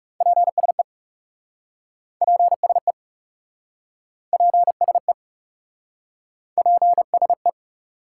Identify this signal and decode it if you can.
Morse code, heard as a keyed tone, about 40 wpm, 700 Hz. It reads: PSE PSE PSE PSE